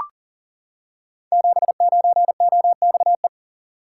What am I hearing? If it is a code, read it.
Morse code, heard as a keyed tone, about 39 wpm, 700 Hz. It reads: 79OXE
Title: Code Groups - 5 Letter:Numbers - 40wpm